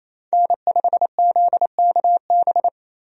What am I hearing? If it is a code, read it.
Morse code, heard as a keyed tone, about 28 wpm, 700 Hz. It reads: N5ZKB
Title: USA Call Signs - 5 Characters - 28wpm